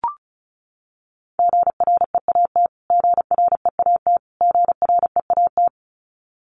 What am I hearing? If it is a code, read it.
Morse code, heard as a keyed tone, about 34 wpm, 700 Hz. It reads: GREAT GREAT GREAT
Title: Top 500 Words Repeated 35wpm